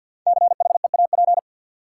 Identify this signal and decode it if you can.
Morse code, heard as a keyed tone, about 53 wpm, 700 Hz. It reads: CHEAP
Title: Sentences from Top 1200 Words - New Word Drill - 50wpm